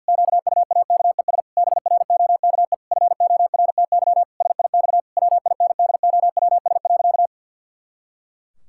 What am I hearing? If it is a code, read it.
Morse code, heard as a keyed tone, about 49 wpm, 700 Hz. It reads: QUAKES BROKE FORTY SIX WINDOWS.
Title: Pangram Challenge v2 - 50wpm